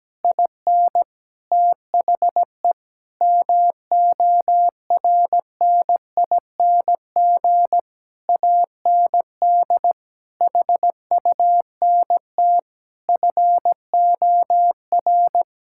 Morse code, heard as a keyed tone, about 17 wpm, 700 Hz. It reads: IN THE MORNING AND HUNT FOR
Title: Sentences from Top 1600 Words 17wpm